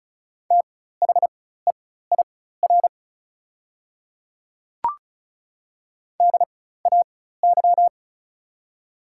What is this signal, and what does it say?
Morse code, heard as a keyed tone, about 36 wpm, spaced as for 16 wpm, 690 Hz. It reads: THEIR DAY
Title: 4x Character Spacing - Top 100 Words - 35wpm